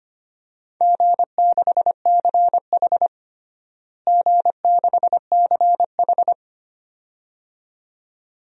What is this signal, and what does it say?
Morse code, heard as a keyed tone, about 25 wpm, 700 Hz. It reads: G6CH G6CH
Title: DXCC-100 Call Signs - 4 Characters - Repeated 2x - 25wpm